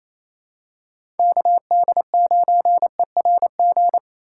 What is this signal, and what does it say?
Morse code, heard as a keyed tone, about 28 wpm, 700 Hz. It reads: KD9ERG
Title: USA Call Signs - 6 Characters - 2x Character Spacing - 28wpm